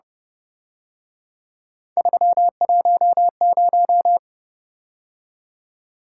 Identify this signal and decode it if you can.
Morse code, heard as a keyed tone, about 28 wpm, 700 Hz. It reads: 310